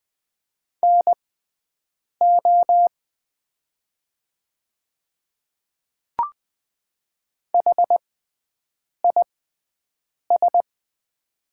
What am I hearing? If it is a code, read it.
Morse code, heard as a keyed tone, about 20 wpm, spaced as for 7 wpm, 700 Hz. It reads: NO HIS
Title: 6x Character Spacing - Top 100 Words - 20wpm